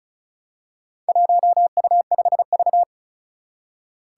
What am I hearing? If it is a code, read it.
Morse code, heard as a keyed tone, about 36 wpm, 700 Hz. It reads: 1U5V
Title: Four Challenging Letter-Number Combinations 35wpm